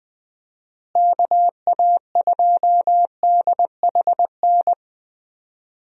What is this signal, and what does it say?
Morse code, heard as a keyed tone, about 20 wpm, 700 Hz. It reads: KA2DHN
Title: USA Call Signs - 6 Characters - Repeated 2x - 20wpm